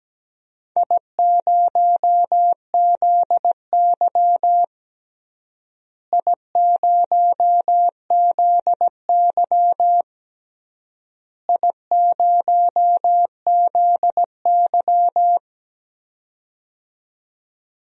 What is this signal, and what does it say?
Morse code, heard as a keyed tone, about 17 wpm, 700 Hz. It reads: I0ZY I0ZY I0ZY
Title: DXCC-100 Call Signs - 4 Characters - Repeated 3x - 17wpm